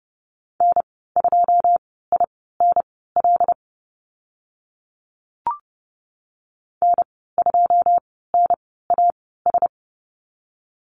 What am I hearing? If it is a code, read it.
Morse code, heard as a keyed tone, about 30 wpm, spaced as for 17 wpm, 700 Hz. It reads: N2INL N2NAS
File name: Lesson 15g - Number 5 - Call Signs 30wpm